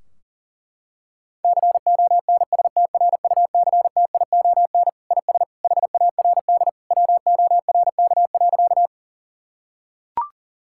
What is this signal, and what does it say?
Morse code, heard as a keyed tone, about 40 wpm, 700 Hz. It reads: CONSTRUCTION IS HARD WORK.
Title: Sentences from Top 1600 Words 40wpm